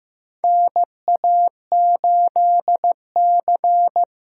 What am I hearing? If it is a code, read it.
Morse code, heard as a keyed tone, about 15 wpm, 700 Hz. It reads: NA8C
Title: USA Callsign DE Callsign - 4 Characters - 15wpm